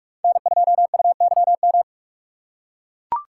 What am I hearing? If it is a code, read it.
Morse code, heard as a keyed tone, about 42 wpm, 700 Hz. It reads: N2UYM